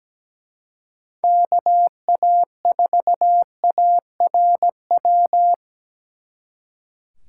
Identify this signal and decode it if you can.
Morse code, heard as a keyed tone, about 21 wpm, 700 Hz. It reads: KA4ARW